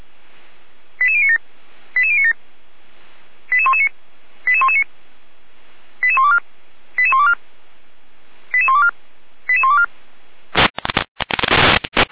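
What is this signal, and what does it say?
Five-tone selective call (ZVEI), heard as quick runs of five tones, tones of about 70 ms. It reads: 88097 88097 88119 88119 88124 88124 88124 88124
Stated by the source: ZVEI five-tone selective call received by SDR.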